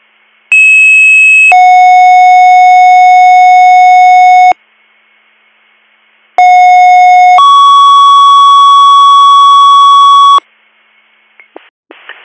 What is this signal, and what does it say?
Two-tone page, heard as a one-second tone, then a three-second tone, about 2600 then 730 Hz, about 730 then 1120 Hz. It is heard from 0.5 to 10.4 s.